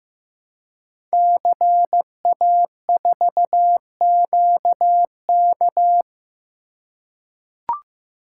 Morse code, heard as a keyed tone, about 15 wpm, 700 Hz. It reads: CA4QK
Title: Code Groups - 5 Letter:Numbers - 15wpm